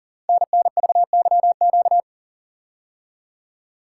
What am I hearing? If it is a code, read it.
Morse code, heard as a keyed tone, about 44 wpm, 700 Hz. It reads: NNVYQ
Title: Code Groups - 5 Letters - 40wpm